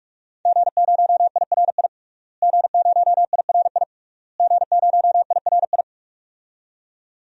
Morse code, heard as a keyed tone, about 43 wpm, 700 Hz. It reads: G0IRI G0IRI G0IRI
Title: DXCC-100 Call Signs - 5 Characters - Repeated 3x - 45wpm